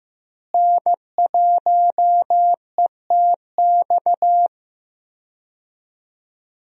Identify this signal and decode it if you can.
Morse code, heard as a keyed tone, about 15 wpm, 700 Hz. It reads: N1ETX
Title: USA Call Signs - 5 Characters - Repeated 3x - 15wpm